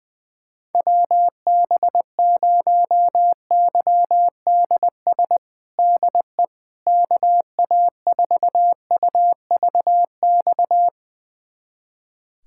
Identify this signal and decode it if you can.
Morse code, heard as a keyed tone, about 20 wpm, 700 Hz. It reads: WB0YDS DE KA4UVX
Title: USA Callsign DE Callsign - 6 Characters - 20wpm